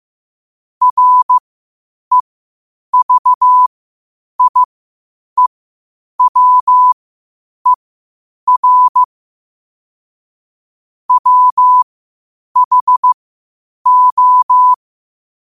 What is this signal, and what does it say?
Morse code, heard as a keyed tone, about 15 wpm, spaced as for 8 wpm, 1000 Hz. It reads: REVIEWER WHO
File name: qotdfiles / 8wpm